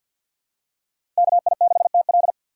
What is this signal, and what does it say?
Morse code, heard as a keyed tone, about 52 wpm, 700 Hz. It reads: KI6TL
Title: USA Call Signs - 5 Characters - 6x Character Spacing - 50wpm